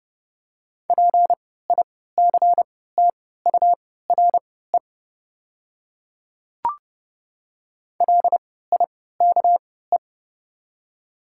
Morse code, heard as a keyed tone, about 30 wpm, spaced as for 17 wpm, 700 Hz. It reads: PICTURE LIKE
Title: 3x Character Spacing - Top 200 Words - 30wpm